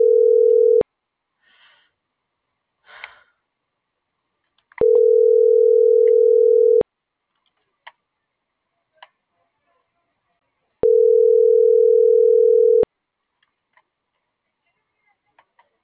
Ringback tone (North America), 440 + 480 Hz.